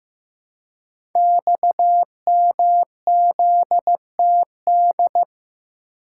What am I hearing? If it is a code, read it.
Morse code, heard as a keyed tone, about 15 wpm, 700 Hz. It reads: XMZTD
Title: Code Groups - 5 Letter:Numbers - 15wpm